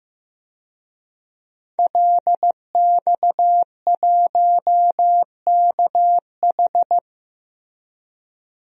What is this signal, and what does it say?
Morse code, heard as a keyed tone, about 15 wpm, 700 Hz. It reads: LX1KH
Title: DXCC-100 Call Signs - 5 Characters - Repeated 3x - 15wpm